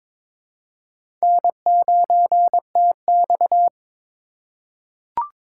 Morse code, heard as a keyed tone, about 23 wpm, 700 Hz. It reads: N9TX
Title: USA Call Signs - 4 Characters - 3x Character Spacing - 22wpm